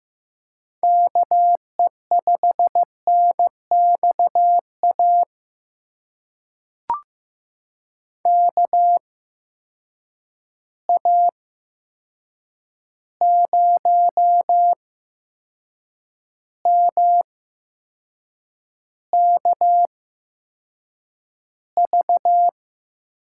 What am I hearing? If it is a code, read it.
Morse code, heard as a keyed tone, about 15 wpm, spaced as for 6 wpm, 700 Hz. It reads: KE5NXA KA0MKV
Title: USA Call Signs - 6 Characters - 8x Character Spacing - 15wpm